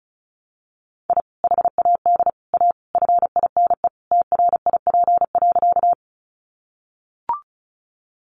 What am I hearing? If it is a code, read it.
Morse code, heard as a keyed tone, about 34 wpm, 700 Hz. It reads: I HAD A FINE TRIP.
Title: Sentences from Top 1300 Words 35wpm